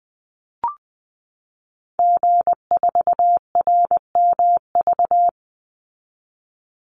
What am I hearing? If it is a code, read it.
Morse code, heard as a keyed tone, about 20 wpm, 700 Hz. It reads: G4RMV
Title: Call Signs Repeated 3x - 20wpm